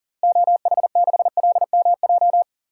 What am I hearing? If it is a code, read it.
Morse code, heard as a keyed tone, about 37 wpm, 700 Hz. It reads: OH6LMJ
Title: DXCC-100 Call Signs - 6 Characters - Repeated 3x - 40wpm